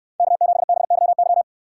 Morse code, heard as a keyed tone, about 69 wpm, 700 Hz. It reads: K7BYY